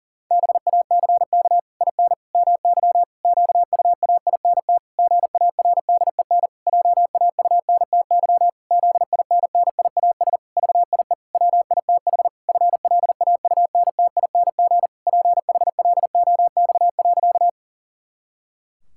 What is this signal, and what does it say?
Morse code, heard as a keyed tone, about 40 wpm, 700 Hz. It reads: BACK IN MY QUAINT GARDEN JAUNTY ZINNIAS VIE WITH FLAUNTING PHLOX.
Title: Pangram Challenge v2 - 40wpm